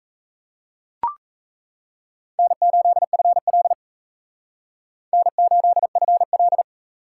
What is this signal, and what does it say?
Morse code, heard as a keyed tone, about 35 wpm, 700 Hz. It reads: N8FL N8FL
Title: Speed Racing - USA Call Signs 28wpm